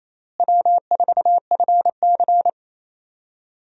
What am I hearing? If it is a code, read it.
Morse code, heard as a keyed tone, about 28 wpm, 700 Hz. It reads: W4FC
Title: USA Call Signs - 4 Characters - Repeated 2x - 28wpm